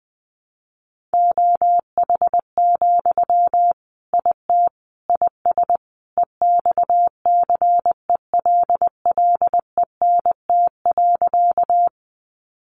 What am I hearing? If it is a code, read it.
Morse code, heard as a keyed tone, about 20 wpm, 700 Hz. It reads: OH, IT IS EXCELLENT.
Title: Sentences from Top 1000 Words 20wpm